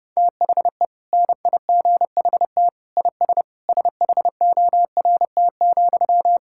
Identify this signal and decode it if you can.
Morse code, heard as a keyed tone, about 30 wpm, 700 Hz. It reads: THE NIGHT IS SHORT,
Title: Sentences from Top 1600 Words 30wpm